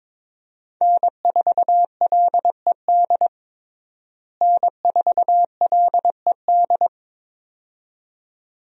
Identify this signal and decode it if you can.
Morse code, heard as a keyed tone, about 22 wpm, 700 Hz. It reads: N4LED N4LED
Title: USA Call Signs - 5 Characters - Repeated 2x - 22wpm